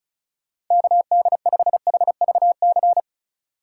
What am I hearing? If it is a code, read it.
Morse code, heard as a keyed tone, about 45 wpm, 700 Hz. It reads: KD5HVC